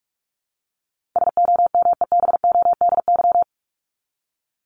Morse code, heard as a keyed tone, about 44 wpm, 700 Hz. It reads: SOMEBODY
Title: Top 1751-2000 Words 45wpm